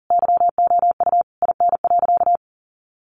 Morse code, heard as a keyed tone, about 40 wpm, 700 Hz. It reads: YOU IN.
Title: Sentences from Top 1500 Words 40wpm